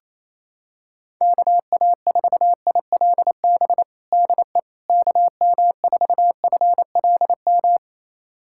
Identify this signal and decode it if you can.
Morse code, heard as a keyed tone, about 28 wpm, 700 Hz. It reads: KA4ILB DE KM4FLM
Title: USA Callsign DE Callsign 28wpm